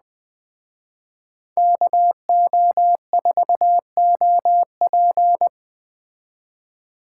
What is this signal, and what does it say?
Morse code, heard as a keyed tone, about 20 wpm, 700 Hz. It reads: KO4OP
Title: Speed Racing - USA 2x2 Call Signs 17wpm